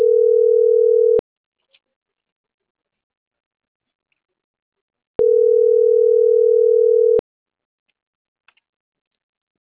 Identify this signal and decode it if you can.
Ringback tone (North America), 440 + 480 Hz.